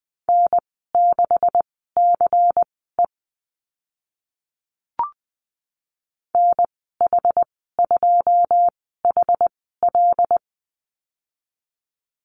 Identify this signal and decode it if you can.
Morse code, heard as a keyed tone, about 20 wpm, spaced as for 14 wpm, 700 Hz. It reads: N6CE NH2HL
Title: 140 Lesson 23g - Letter f - Call Signs 20wpm